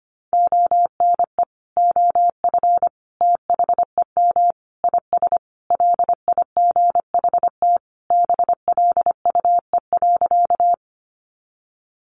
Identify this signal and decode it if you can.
Morse code, heard as a keyed tone, about 25 wpm, 700 Hz. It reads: ONE OF THEM IS LIGHT BLUE.
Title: Additional Easy Phrases 25wpm